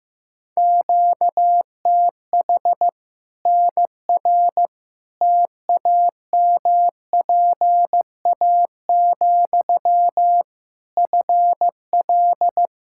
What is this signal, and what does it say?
Morse code, heard as a keyed tone, about 15 wpm, 700 Hz. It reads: QTH NR TAMPA, FL
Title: Practice QSO - Exchange 1 - Beginning - 15wpm